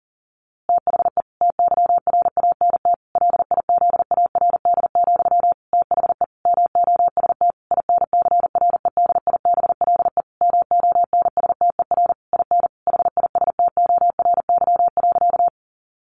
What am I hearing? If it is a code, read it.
Morse code, heard as a keyed tone, about 40 wpm, 700 Hz. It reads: THE TYRANT LIZARD, THE MOST INCREDIBLE MONSTER IN HISTORY.
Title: A Sound Of Thunder 40wpm